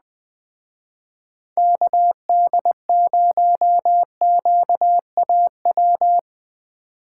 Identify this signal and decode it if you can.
Morse code, heard as a keyed tone, about 20 wpm, 700 Hz. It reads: KD0QAW